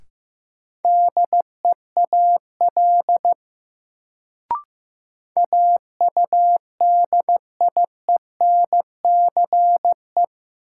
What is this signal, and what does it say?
Morse code, heard as a keyed tone, about 15 wpm, 700 Hz. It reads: DEAL AUDIENCE
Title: Sentences from Top 800 Words - New Word Drill - 15wpm